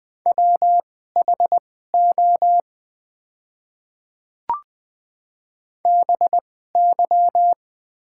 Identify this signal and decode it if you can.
Morse code, heard as a keyed tone, about 20 wpm, spaced as for 14 wpm, 700 Hz. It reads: WHO BY
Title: 2x Character Spacing - Top 100 Words - 20wpm